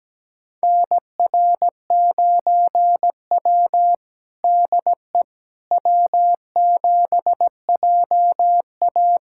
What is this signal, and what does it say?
Morse code, heard as a keyed tone, about 17 wpm, 700 Hz. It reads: NR9W DE W7JA
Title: USA Callsign DE Callsign - 4 Characters - 17wpm